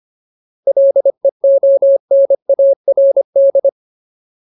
Morse code, heard as a keyed tone, about 25 wpm, 550 Hz. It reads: LEONARD
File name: Ultimate Word List / 25WPM Ultimate